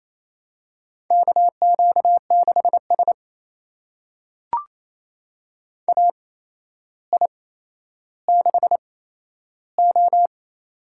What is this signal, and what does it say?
Morse code, heard as a keyed tone, about 27 wpm, spaced as for 12 wpm, 700 Hz. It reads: KQ6S AI6O
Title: USA Call Signs - 4 Characters - 8x Character Spacing - 28wpm